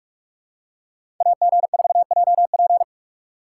Morse code, heard as a keyed tone, about 46 wpm, 700 Hz. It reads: AG4JP
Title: USA Call Signs - 5 Characters - 2x Character Spacing - 45wpm